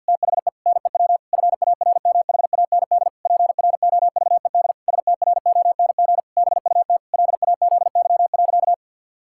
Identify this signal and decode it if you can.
Morse code, heard as a keyed tone, about 50 wpm, 700 Hz. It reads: THE NEW FARMHAND PROVED STRONG BUT LAZY.
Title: Pangram Challenge v2 - 50wpm